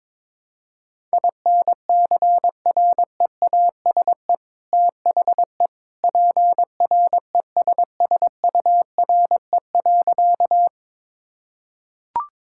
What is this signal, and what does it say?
Morse code, heard as a keyed tone, about 22 wpm, 700 Hz. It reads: INCREASE THE PRESSURE.
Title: Sentences from Top 1000 Words - Review Part 6 - 22wpm